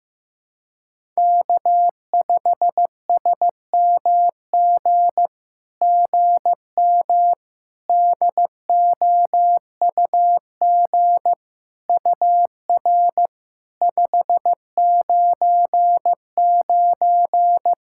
Morse code, heard as a keyed tone, about 15 wpm, 700 Hz. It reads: K5SMG GM DOUG UR 599
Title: SOTA QSO Practice 15wpm